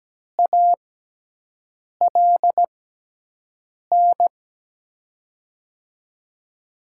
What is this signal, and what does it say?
Morse code, heard as a keyed tone, about 17 wpm, spaced as for 6 wpm, 700 Hz. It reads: ALN
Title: USA Call Signs - 6 Characters - 6x Character Spacing - 17wpm